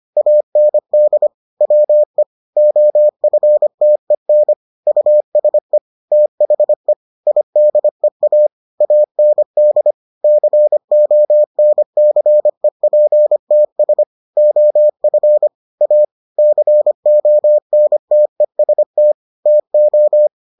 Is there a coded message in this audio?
Yes: Morse code from 0.2 to 20.3 s, heard as a keyed tone, about 25 wpm, 600 Hz. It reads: AND WE OFTEN USE THE IDEA AND CONCEPTS OF A CONTEST TO